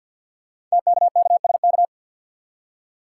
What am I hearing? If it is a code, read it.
Morse code, heard as a keyed tone, about 47 wpm, 700 Hz. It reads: TKKSK
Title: Code Groups - 5 Letters - 50wpm